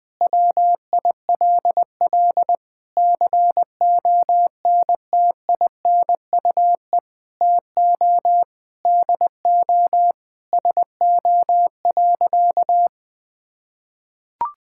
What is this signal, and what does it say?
Morse code, heard as a keyed tone, about 21 wpm, 700 Hz. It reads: WILL CONTINUE TO DO SO.
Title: Sentences with 6-Word Phrases - Spoken Beginning - 20wpm